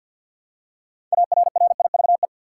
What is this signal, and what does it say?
Morse code, heard as a keyed tone, about 50 wpm, 700 Hz. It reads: ARRIVE
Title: Six Letter Words - 50wpm